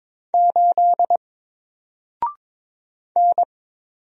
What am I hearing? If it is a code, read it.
Morse code, heard as a keyed tone, about 22 wpm, 700 Hz. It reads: 8 N
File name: Single-Letter-Number-22wpm